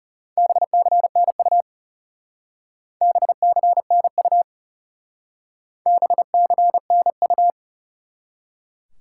Morse code, heard as a keyed tone, about 37 wpm, 700 Hz. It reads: BCNU BCNU BCNU
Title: Speed-Racing Top 90 QSO Elements 30wpm